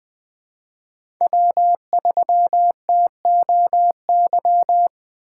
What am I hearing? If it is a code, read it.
Morse code, heard as a keyed tone, about 20 wpm, 700 Hz. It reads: W3TOY
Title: USA Call Signs - 4x Character Spacing - 20wpm